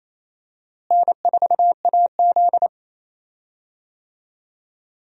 Morse code, heard as a keyed tone, about 28 wpm, 700 Hz. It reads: N4AZ
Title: USA Call Signs - 4 Characters - Repeated 2x - 28wpm